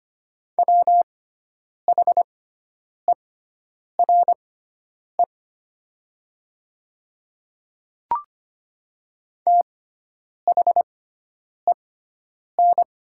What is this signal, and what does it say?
Morse code, heard as a keyed tone, about 25 wpm, spaced as for 9 wpm, 700 Hz. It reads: WHERE THEN
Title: 6x Character Spacing - Top 300 Words - 25wpm